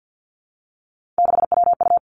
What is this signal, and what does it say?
Morse code, heard as a keyed tone, about 50 wpm, 700 Hz. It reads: 6WU
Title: Three Challenging Letter-Number Combinations 50wpm